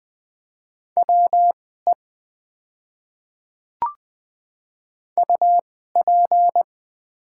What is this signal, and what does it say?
Morse code, heard as a keyed tone, about 20 wpm, spaced as for 14 wpm, 700 Hz. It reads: WE UP
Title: 2x Character Spacing - Top 100 Words - 20wpm